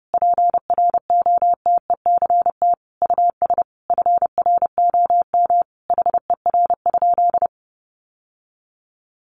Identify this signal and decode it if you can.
Morse code, heard as a keyed tone, about 30 wpm, 700 Hz. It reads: PROTECT US FROM HER?
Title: Sentences from Top 900 Words 30wpm